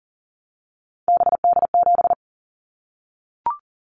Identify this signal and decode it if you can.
Morse code, heard as a keyed tone, about 40 wpm, 700 Hz. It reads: BD7
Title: Callsign Prefixes for Top 100 DXCC - 40wpm